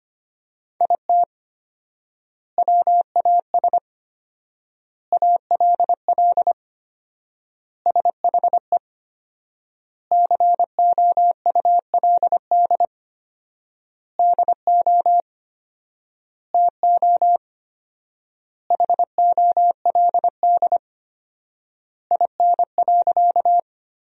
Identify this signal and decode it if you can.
Morse code, heard as a keyed tone, about 25 wpm, 700 Hz. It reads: IT WAS ALL SHE COULD DO TO HOLD IN.
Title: 4x Word-Spacing - Sentences from Top 400 Words - 25wpm